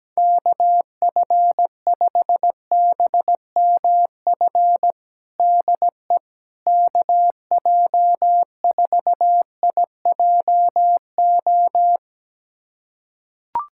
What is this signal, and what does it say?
Morse code, heard as a keyed tone, about 17 wpm, 700 Hz. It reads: KF5BMF DE KJ4IJO
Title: USA Callsign DE Callsign - 6 Characters - 17wpm